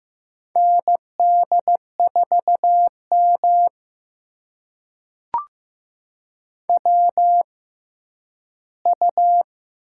Morse code, heard as a keyed tone, about 15 wpm, spaced as for 10 wpm, 700 Hz. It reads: ND4M WU
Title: USA Call Signs - 4 Characters - 6x Character Spacing - 15wpm